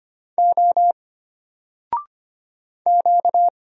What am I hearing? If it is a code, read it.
Morse code, heard as a keyed tone, about 25 wpm, 700 Hz. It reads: O Q
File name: Single-Letter-Number-25wpm